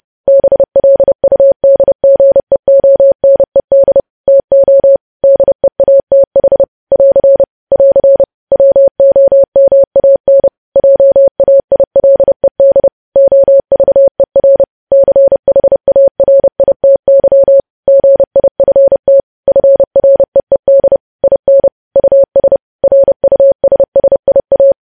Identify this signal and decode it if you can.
Morse code, heard as a keyed tone, about 30 wpm, 550 Hz. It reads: BLUDGEONED TO DEATH <AR> <AR> WOMAN JAILED OVER CHARITY GIFT FREED IN US RUSSIA
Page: news headlines in morse code at 30 wpm